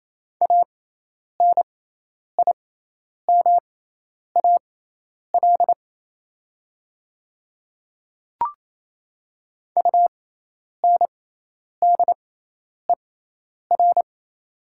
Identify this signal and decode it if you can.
Morse code, heard as a keyed tone, about 28 wpm, spaced as for 10 wpm, 700 Hz. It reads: ANIMAL UNDER
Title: 6x Character Spacing - Top 200 Words - 28wpm